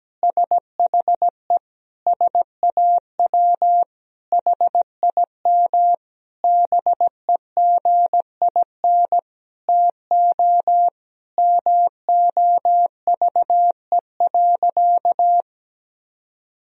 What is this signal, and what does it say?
Morse code, heard as a keyed tone, about 17 wpm, 700 Hz. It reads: SHE SAW HIM BEGIN TO MOVE.
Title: Sentences from Top 1000 Words - Review Part 2 - 17wpm